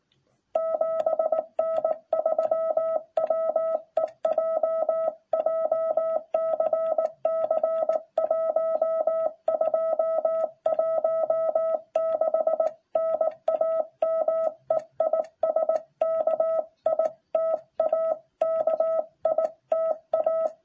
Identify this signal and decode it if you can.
Morse code, heard as a keyed tone, about 16 wpm, 660 Hz. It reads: 7N3WEJ JCC 1216 NAME IS KITA KITA